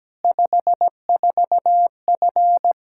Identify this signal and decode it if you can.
Morse code, heard as a keyed tone, about 17 wpm, 700 Hz. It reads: 54F
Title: Practice QSO - Exchange 2 - Info - 17wpm